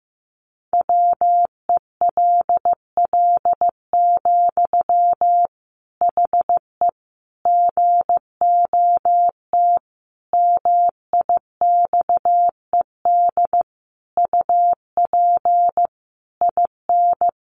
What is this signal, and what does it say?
Morse code, heard as a keyed tone, about 15 wpm, 700 Hz. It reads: WELL, HE GOT MIXED UP IN
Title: Sentences from Top 1500 Words 15wpm